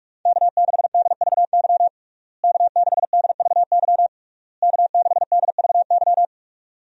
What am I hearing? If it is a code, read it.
Morse code, heard as a keyed tone, about 41 wpm, 700 Hz. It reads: K6DVY K6DVY K6DVY
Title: USA Call Signs - 5 Characters - Repeated 3x - 45wpm